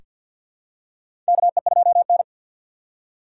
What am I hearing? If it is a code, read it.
Morse code, heard as a keyed tone, about 47 wpm, 700 Hz. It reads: KE2N
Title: USA Call Signs - 3x Character Spacing - 50wpm